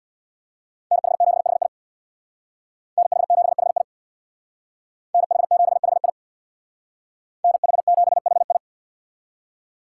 Morse code, heard as a keyed tone, about 62 wpm, 700 Hz. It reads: NH7HI NH7HI NH7HI NH7HI
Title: Speed Racing - USA Call Signs 50wpm